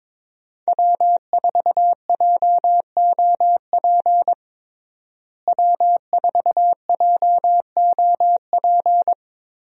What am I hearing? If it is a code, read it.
Morse code, heard as a keyed tone, about 22 wpm, 700 Hz. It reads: W4JOP W4JOP
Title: USA Call Signs - 5 Characters - Repeated 3x - 22wpm